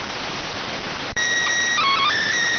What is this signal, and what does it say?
SSTV (slow-scan television), PD 290.